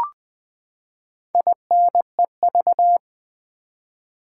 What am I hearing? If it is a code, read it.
Morse code, heard as a keyed tone, about 20 wpm, 700 Hz. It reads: INEV
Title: Build A Word - 4 Letters At A Time - 10-Letter Words - 20wpm